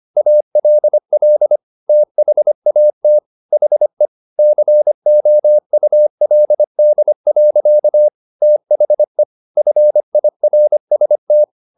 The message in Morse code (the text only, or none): ALL THAT HE COULD. THE FIRST